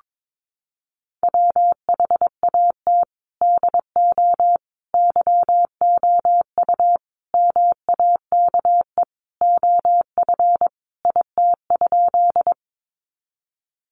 Morse code, heard as a keyed tone, about 22 wpm, 700 Hz. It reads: WHAT DO YOU MAKE OF IT?
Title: Sentences from Top 100 Words 22wpm